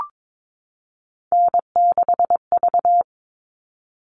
Morse code, heard as a keyed tone, about 22 wpm, 700 Hz. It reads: N6V
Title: All Cut Numbers - 3-Digit 22wpm